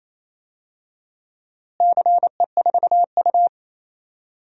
Morse code, heard as a keyed tone, about 28 wpm, 700 Hz. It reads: CE4U
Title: DXCC-100 Call Signs - 4 Characters - Repeated 2x - 28wpm